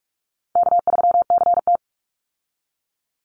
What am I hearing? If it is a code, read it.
Morse code, heard as a keyed tone, about 44 wpm, 700 Hz. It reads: K3CT
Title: Call Signs Repeated 3x - 45wpm